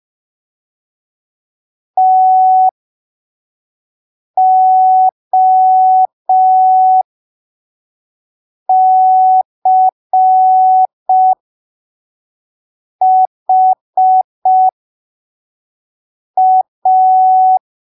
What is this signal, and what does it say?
Morse code, heard as a keyed tone, about 5 wpm, spaced as for 3 wpm, 750 Hz. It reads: TOCHA
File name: arquivos / texto / texto 08 05ppm